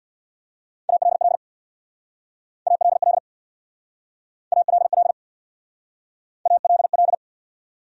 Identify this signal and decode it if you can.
Morse code, heard as a keyed tone, about 67 wpm, 700 Hz. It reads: ALL ALL ALL ALL